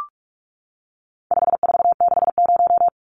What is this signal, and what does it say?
Morse code, heard as a keyed tone, about 44 wpm, 700 Hz. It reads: 5460
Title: Numbers - 4-Digit 45wpm